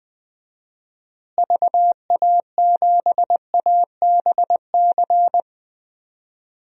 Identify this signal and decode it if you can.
Morse code, heard as a keyed tone, about 20 wpm, 700 Hz. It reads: VA7ABC
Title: DXCC-100 Call Signs - 6 Characters - Repeated 2x - 20wpm